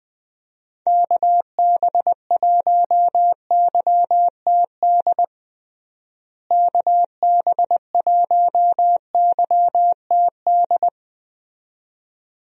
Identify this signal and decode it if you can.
Morse code, heard as a keyed tone, about 21 wpm, 700 Hz. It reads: KB1YTD KB1YTD
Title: USA Call Signs - 6 Characters - Repeated 2x - 20wpm